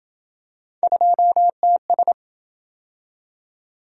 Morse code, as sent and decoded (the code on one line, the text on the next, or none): ..--- - ...
2TS